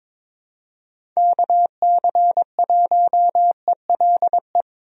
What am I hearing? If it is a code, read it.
Morse code, heard as a keyed tone, about 22 wpm, 700 Hz. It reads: KC1ELE
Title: USA Call Signs - 6 Characters - 22wpm